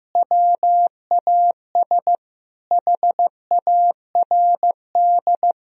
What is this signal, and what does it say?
Morse code, heard as a keyed tone, about 16 wpm, 700 Hz. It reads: WAS HARD
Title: Sentences from Top 1000 Words - Review Part 1 - 15wpm